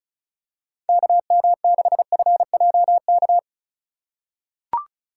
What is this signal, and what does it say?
Morse code, heard as a keyed tone, about 36 wpm, 700 Hz. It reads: KM6FJK
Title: USA Call Signs - 4x Character Spacing - 35wpm